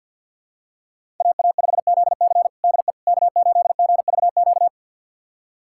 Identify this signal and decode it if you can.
Morse code, heard as a keyed tone, about 52 wpm, 700 Hz. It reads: AA5ZC DE K8GVQ
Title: USA Callsign DE Callsign - 5 Characters - 50wpm